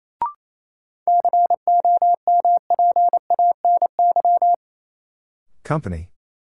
Morse code, heard as a keyed tone, about 28 wpm, 700 Hz. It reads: COMPANY
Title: Sets of 2 Words - New Word Drill - 28wpm